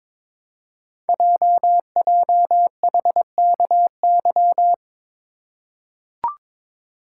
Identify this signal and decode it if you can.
Morse code, heard as a keyed tone, about 22 wpm, 700 Hz. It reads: JJHKY
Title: Code Groups - 5 Letters - 22wpm